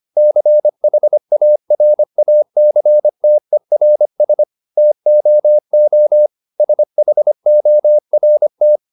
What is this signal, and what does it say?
Morse code, heard as a keyed tone, about 25 wpm, 600 Hz. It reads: CHARACTERS TOO SHORT